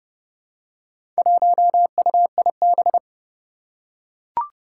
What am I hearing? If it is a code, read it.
Morse code, heard as a keyed tone, about 30 wpm, 700 Hz. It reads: 1UIB
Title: Four Challenging Letter-Number Combinations 30wpm